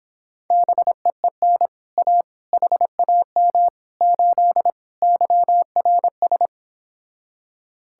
Morse code, heard as a keyed tone, about 26 wpm, 700 Hz. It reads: BEEN A HAM 8 YRS